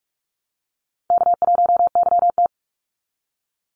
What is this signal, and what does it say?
Morse code, heard as a keyed tone, about 44 wpm, 700 Hz. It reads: K1YT
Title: Call Signs Repeated 3x - 45wpm